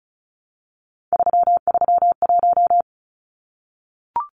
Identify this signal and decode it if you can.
Morse code, heard as a keyed tone, about 34 wpm, 700 Hz. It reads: 331
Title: Numbers - 3-Digit 35wpm